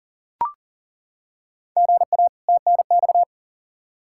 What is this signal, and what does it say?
Morse code, heard as a keyed tone, about 40 wpm, 700 Hz. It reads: GA TNX
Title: Practice QSO - Exchange 1 - Beginning - 40wpm